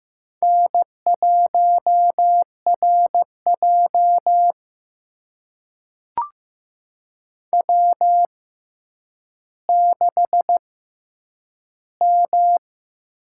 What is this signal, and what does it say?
Morse code, heard as a keyed tone, about 15 wpm, spaced as for 9 wpm, 700 Hz. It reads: N1RJ W6M
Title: USA Call Signs - 4 Characters - 6x Character Spacing - 15wpm